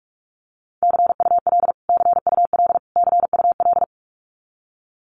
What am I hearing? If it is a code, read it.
Morse code, heard as a keyed tone, about 44 wpm, 700 Hz. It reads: CUL CUL CUL
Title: Abbreviations Repeated 3x - 45wpm